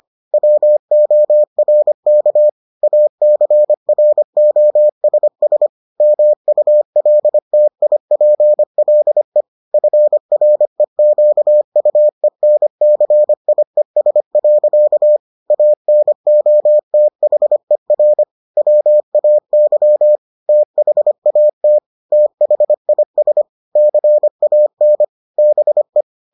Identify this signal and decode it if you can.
Morse code, heard as a keyed tone, about 25 wpm, 600 Hz. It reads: WORK ACROSS MULTIPLE FREQUENCIES. ANOTHER WAY THAT THIS CAN BE